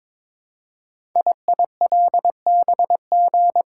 Morse code, heard as a keyed tone, about 22 wpm, 700 Hz. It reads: IILBG
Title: Code Groups - 5 Letters - 22wpm